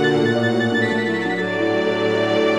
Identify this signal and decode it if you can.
There is no code.